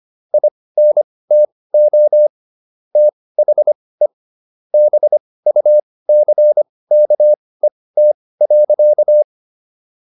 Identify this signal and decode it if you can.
Morse code, heard as a keyed tone, about 25 wpm, spaced as for 18 wpm, 600 Hz. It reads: INTO THE BUCKET.